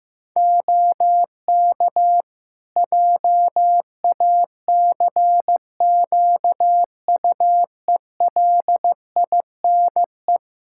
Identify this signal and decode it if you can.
Morse code, heard as a keyed tone, about 15 wpm, 700 Hz. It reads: OK JACQUELINE
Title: K1USN SST QSO Practice 15wpm